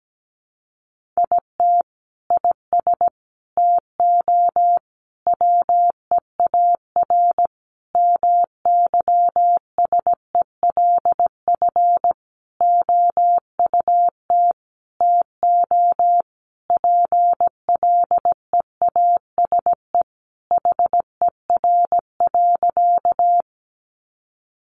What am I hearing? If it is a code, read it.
Morse code, heard as a keyed tone, about 17 wpm, 700 Hz. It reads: IT IS TO WEAR MYSELF OUT TO PLEASE HER.
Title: Sentences from Top 1500 Words 17wpm